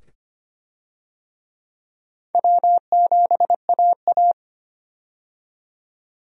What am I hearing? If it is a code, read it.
Morse code, heard as a keyed tone, about 25 wpm, 700 Hz. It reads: W7AA
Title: USA Call Signs - 4 Characters - 25wpm